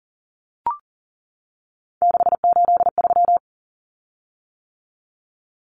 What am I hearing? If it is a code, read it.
Morse code, heard as a keyed tone, about 40 wpm, 700 Hz. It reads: 683
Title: Common Cut Numbers - 3-Digit 40wpm